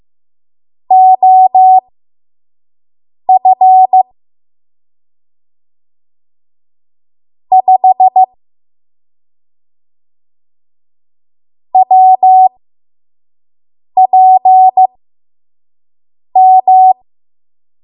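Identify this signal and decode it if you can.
Morse code, heard as a keyed tone, about 15 wpm, spaced as for 5 wpm, 750 Hz. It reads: OF 5 WPM
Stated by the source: 5 WPM Morse code practice.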